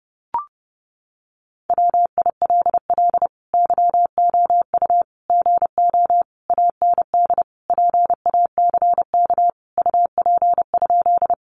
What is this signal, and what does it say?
Morse code, heard as a keyed tone, about 30 wpm, 700 Hz. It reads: WILL YOU GO AND PACK UP?
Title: Sentences from Top 1300 Words 30wpm